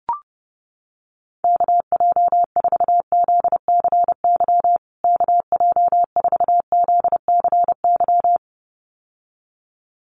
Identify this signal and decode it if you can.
Morse code, heard as a keyed tone, about 30 wpm, 700 Hz. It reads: KJ4ZCY KJ4ZCY
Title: USA Call Signs Repeated 2x 30wpm